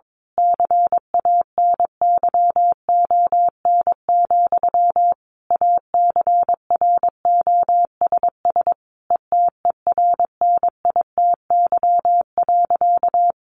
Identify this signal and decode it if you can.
Morse code, heard as a keyed tone, about 22 wpm, 700 Hz. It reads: CANYON, ACROSS ETERNITY.
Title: A Sound Of Thunder 22wpm